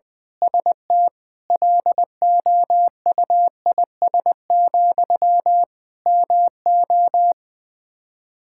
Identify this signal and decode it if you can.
Morse code, heard as a keyed tone, about 21 wpm, 700 Hz. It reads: ST LOUIS, MO
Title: Practice QSO - Exchange 1 - Beginning - 20wpm